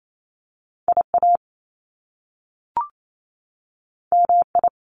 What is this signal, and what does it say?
Morse code, heard as a keyed tone, about 28 wpm, 700 Hz. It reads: IA MI
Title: Top Two Letter Combinations in Words 28wpm